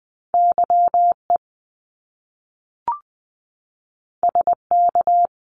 Morse code, heard as a keyed tone, about 20 wpm, 700 Hz. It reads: YE SK
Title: Top Two Letter Combinations in Words 20wpm